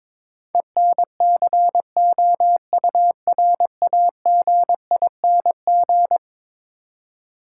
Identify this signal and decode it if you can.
Morse code, heard as a keyed tone, about 22 wpm, 700 Hz. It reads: ENCOURAGING
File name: practice / 22wpm / Eleven Letter Words - 22wpm